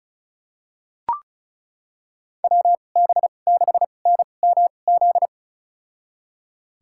Morse code, heard as a keyed tone, about 36 wpm, spaced as for 25 wpm, 700 Hz. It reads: WB6NMZ
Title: USA Call Signs - 6 Characters - 2x Character Spacing - 35wpm